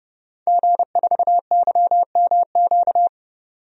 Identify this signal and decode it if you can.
Morse code, heard as a keyed tone, about 30 wpm, 700 Hz. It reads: G4YMQ
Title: DXCC-100 Call Signs - 5 Characters - 30wpm